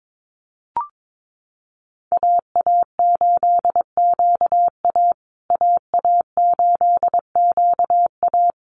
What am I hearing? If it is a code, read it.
Morse code, heard as a keyed tone, about 22 wpm, 700 Hz. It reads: AA8QA AA8QA
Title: USA Call Signs Repeated 3x 22wpm